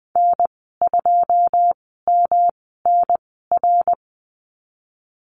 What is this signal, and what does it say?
Morse code, heard as a keyed tone, about 20 wpm, spaced as for 15 wpm, 700 Hz. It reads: N2MNR
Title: 140 Lesson 23g - Letter f - Call Signs 20wpm